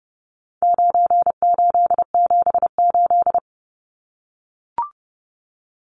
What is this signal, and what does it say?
Morse code, heard as a keyed tone, about 30 wpm, 700 Hz. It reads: 9878
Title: Numbers - 4-Digit 30wpm